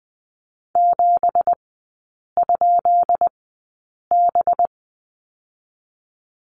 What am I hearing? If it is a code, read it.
Morse code, heard as a keyed tone, about 20 wpm, spaced as for 8 wpm, 700 Hz. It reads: 7?B
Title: 184 Lesson 30e - Letter b - Three Characters 20wpm